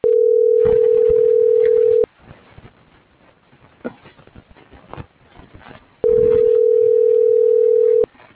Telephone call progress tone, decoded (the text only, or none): ringback tone